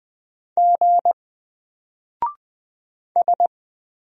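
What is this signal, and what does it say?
Morse code, heard as a keyed tone, about 20 wpm, 700 Hz. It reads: G S
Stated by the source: Single letter and number Morse drill at 20 wpm, from N4LCD.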